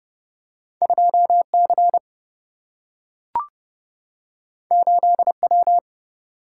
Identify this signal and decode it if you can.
Morse code, heard as a keyed tone, about 30 wpm, 700 Hz. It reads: 2C 8W
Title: Single-Letter Single-Number 30wpm